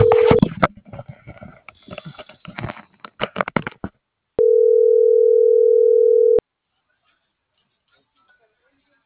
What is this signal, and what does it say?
Ringback tone (North America), 440 + 480 Hz.
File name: CENTRAL MAERSK / out sandovalmsk